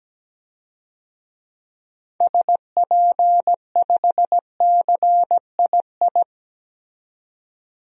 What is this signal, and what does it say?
Morse code, heard as a keyed tone, about 17 wpm, 700 Hz. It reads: SP5CII
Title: DXCC-100 Call Signs - 6 Characters - 17wpm